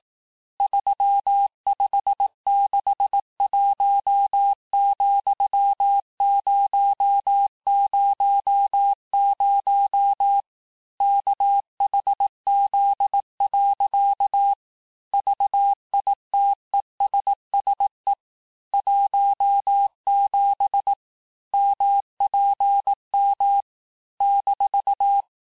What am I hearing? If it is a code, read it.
Morse code, heard as a keyed tone, about 18 wpm, 800 Hz. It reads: 3561,000 KHZ. VITESSE 17 MPM -
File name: cours-du-lundi files / cours-1588-texte-chiffre-vitesse-17MPM